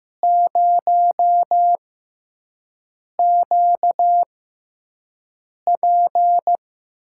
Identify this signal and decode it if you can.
Morse code, heard as a keyed tone, about 15 wpm, spaced as for 5 wpm, 700 Hz. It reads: 0QP
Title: USA Call Signs - 6x Character Spacing - 15wpm